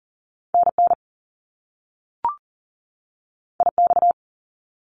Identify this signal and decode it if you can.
Morse code, heard as a keyed tone, about 40 wpm, 700 Hz. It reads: NN IX
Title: Top Two Letter Combinations in Words 40wpm